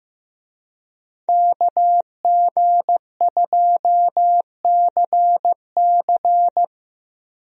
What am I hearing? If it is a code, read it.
Morse code, heard as a keyed tone, about 15 wpm, 700 Hz. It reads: KG2CC
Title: USA Call Signs - 5 Characters - Repeated 2x - 15wpm